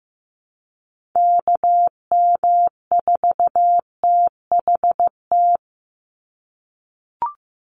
Morse code, heard as a keyed tone, about 15 wpm, 700 Hz. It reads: KM4THT
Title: USA Call Signs 15wpm